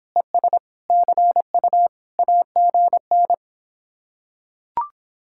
Morse code, heard as a keyed tone, about 26 wpm, 700 Hz. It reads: ES CU AGN